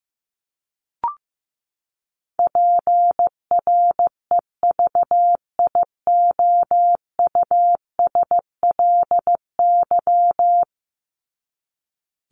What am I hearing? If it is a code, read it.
Morse code, heard as a keyed tone, about 15 wpm, 700 Hz. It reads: PREVIOUSLY
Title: LY Suffix - Top 100 Words with Base Words 15wpm